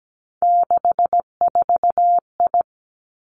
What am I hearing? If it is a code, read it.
Morse code, heard as a keyed tone, about 17 wpm, 700 Hz. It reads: 64I
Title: Three Challenging Letter-Number Combinations 17wpm